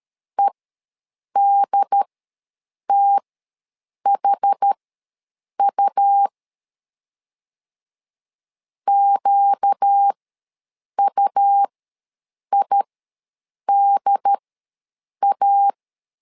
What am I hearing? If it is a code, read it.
Morse code, heard as a keyed tone, about 13 wpm, spaced as for 7 wpm, 800 Hz. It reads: EDTHU QUIDA
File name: LEZIONI / 4 Q ws 12wpm 1912